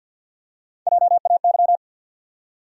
Morse code, heard as a keyed tone, about 50 wpm, 700 Hz. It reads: JAY
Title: USA Call Sign Postfixes - 3 Letters 50wpm